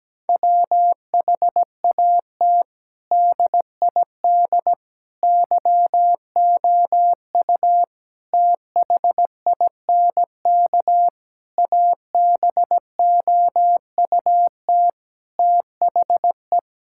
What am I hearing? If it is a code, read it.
Morse code, heard as a keyed tone, about 17 wpm, 700 Hz. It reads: WHAT DID YOU THINK ABOUT THE
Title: Sentences from Top 1000 Words - Review Part 5 - 17wpm